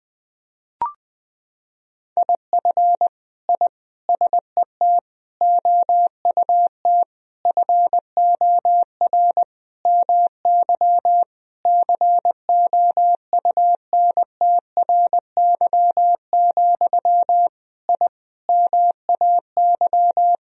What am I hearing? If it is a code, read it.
Morse code, heard as a keyed tone, about 20 wpm, 700 Hz. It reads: IF I SET OUT FOR MY COUNTRY, I MAY
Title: Sentences from Top 1000 Words - Review Part 2 - 20wpm